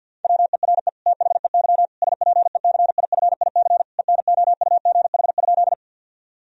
Morse code, heard as a keyed tone, about 52 wpm, 700 Hz. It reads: WERE THEY SPECIFIC ENOUGH?
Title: Sentences from Top 1000 Words - Review Part 3 - 50wpm